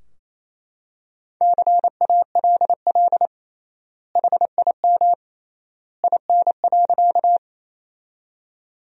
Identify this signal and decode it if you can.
Morse code, heard as a keyed tone, about 28 wpm, 700 Hz. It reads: CALL HIM IN.
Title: 3x Word-Spacing - Sentences from Top 100 Words - 28wpm